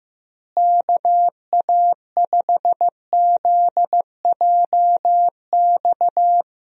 Morse code, heard as a keyed tone, about 16 wpm, 700 Hz. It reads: KA5ZJX
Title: USA Call Signs - 6 Characters - Repeated 3x - 15wpm